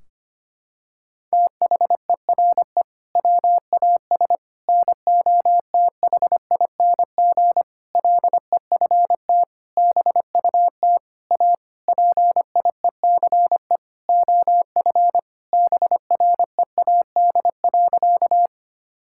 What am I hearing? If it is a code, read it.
Morse code, heard as a keyed tone, about 25 wpm, 700 Hz. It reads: THERE WAS NOTHING LEFT BUT A PIECE OF BREAD.
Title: Sentences from Top 1600 Words 25wpm